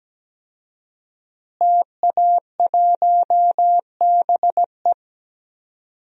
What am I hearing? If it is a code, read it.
Morse code, heard as a keyed tone, about 17 wpm, 700 Hz. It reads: TA1BE
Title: DXCC-100 Call Signs - 5 Characters - Repeated 3x - 17wpm